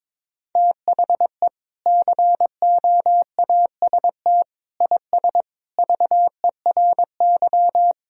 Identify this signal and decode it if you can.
Morse code, heard as a keyed tone, about 22 wpm, 700 Hz. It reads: THE COAST IS VERY
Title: Sentences from Top 1000 Words - Review Part 4 - 22wpm